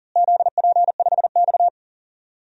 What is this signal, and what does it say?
Morse code, heard as a keyed tone, about 40 wpm, 700 Hz. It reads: ZP5X
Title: DXCC-100 Call Signs - 4 Characters - Repeated 2x - 40wpm